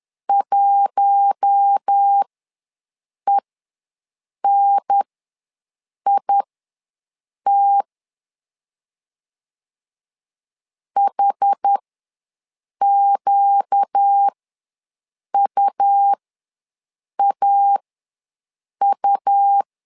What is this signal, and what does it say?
Morse code, heard as a keyed tone, about 10 wpm, spaced as for 6 wpm, 800 Hz. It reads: 1ENIT HQUAU
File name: LEZIONI / 4 Q1 ws 10wpm 2538